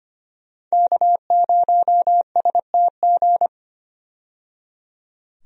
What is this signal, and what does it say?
Morse code, heard as a keyed tone, about 25 wpm, 700 Hz. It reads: K0STG
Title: USA Call Signs - 5 Characters - 25wpm